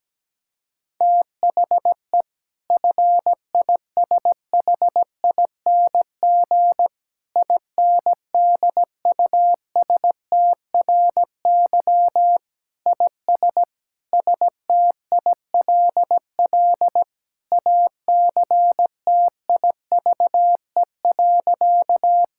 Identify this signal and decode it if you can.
Morse code, heard as a keyed tone, about 17 wpm, 700 Hz. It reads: THE FISHING INDUSTRY IS STILL ACTIVE.
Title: Sentences from Top 1000 Words - Review Part 6 - 17wpm